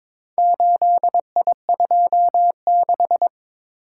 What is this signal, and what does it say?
Morse code, heard as a keyed tone, about 22 wpm, 700 Hz. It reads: 8I26
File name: practice / 22wpm / Four Letter-Number Combinations - 22wpm